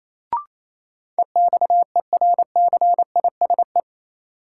Morse code, heard as a keyed tone, about 28 wpm, 700 Hz. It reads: EXERCISE
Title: Sentences from Top 1000 Words - Review Part 4 - New Word Drill - 28wpm